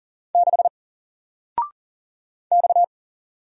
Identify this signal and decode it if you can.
Morse code, heard as a keyed tone, about 40 wpm, 700 Hz. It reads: 6 X
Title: Single Character (Letter:Number:Punctuation:Prosign) Alt 40wpm